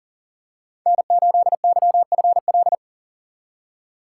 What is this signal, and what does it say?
Morse code, heard as a keyed tone, about 40 wpm, 700 Hz. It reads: N8YFL
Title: USA Call Signs - 5 Characters - 2x Character Spacing - 40wpm